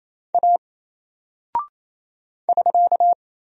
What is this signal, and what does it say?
Morse code, heard as a keyed tone, about 28 wpm, 700 Hz. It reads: A <SK>
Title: Single Character (Letter:Number:Punctuation:Prosign) Alt 28wpm